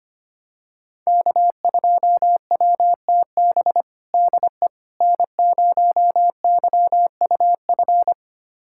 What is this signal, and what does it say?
Morse code, heard as a keyed tone, about 25 wpm, 700 Hz. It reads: K2WTB DE N0YUF
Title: USA Callsign DE Callsign - 5 Characters - 25wpm